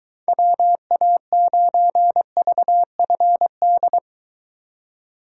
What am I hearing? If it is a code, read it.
Morse code, heard as a keyed tone, about 23 wpm, 700 Hz. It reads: WA9VFD